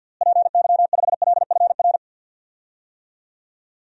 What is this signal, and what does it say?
Morse code, heard as a keyed tone, about 45 wpm, 700 Hz. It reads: PY5LFR
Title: DXCC-100 Call Signs - 6 Characters - Repeated 2x - 50wpm